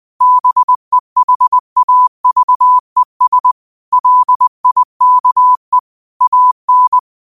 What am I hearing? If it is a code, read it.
Morse code, heard as a keyed tone, about 20 wpm, 1000 Hz. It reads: BEHAVES LIKE AN